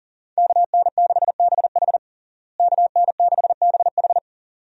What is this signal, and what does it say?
Morse code, heard as a keyed tone, about 40 wpm, 700 Hz. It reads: KN6BH KN6BH
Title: USA Call Signs - 5 Characters - Repeated 2x - 40wpm